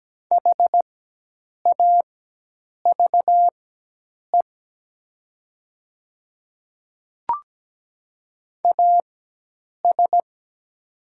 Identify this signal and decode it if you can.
Morse code, heard as a keyed tone, about 17 wpm, spaced as for 8 wpm, 700 Hz. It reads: HAVE AS
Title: 4x Character Spacing - Top 100 Words - 17wpm